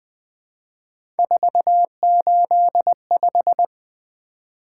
Morse code, heard as a keyed tone, about 20 wpm, 700 Hz. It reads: 485
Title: Three Letter-Number Combinations - 20wpm